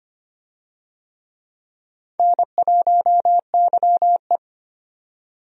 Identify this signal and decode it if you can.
Morse code, heard as a keyed tone, about 25 wpm, 700 Hz. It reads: N1YE
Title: USA Call Signs - 4 Characters - 25wpm